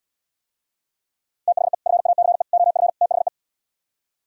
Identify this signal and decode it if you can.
Morse code, heard as a keyed tone, about 77 wpm, 700 Hz. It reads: THE FACE OF THE